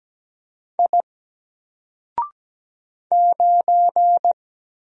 Morse code, heard as a keyed tone, about 17 wpm, 700 Hz. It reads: I 9
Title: Single Letter-Number 17wpm